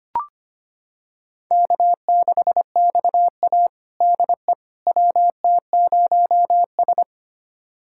Morse code, heard as a keyed tone, about 25 wpm, 700 Hz. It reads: K6XA DE WT0S
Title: USA Callsign DE Callsign - 4 Characters - 25wpm